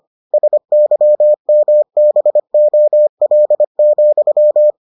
Morse code, heard as a keyed tone, about 25 wpm, 600 Hz. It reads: SYMBOL,